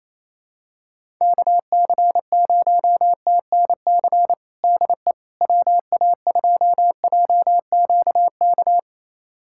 Morse code, heard as a keyed tone, about 27 wpm, 700 Hz. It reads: KC0TNC DE WA2JQK
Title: USA Callsign DE Callsign - 6 Characters - 28wpm